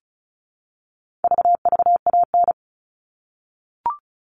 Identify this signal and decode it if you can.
Morse code, heard as a keyed tone, about 34 wpm, 700 Hz. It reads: VVAN
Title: All Cut Numbers - 4-Digit 35wpm